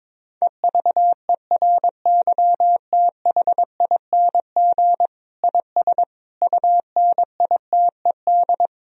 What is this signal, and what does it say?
Morse code, heard as a keyed tone, about 22 wpm, 700 Hz. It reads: EVERYTHING IS UNITED
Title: Sentences from Top 1000 Words - Review Part 4 - 22wpm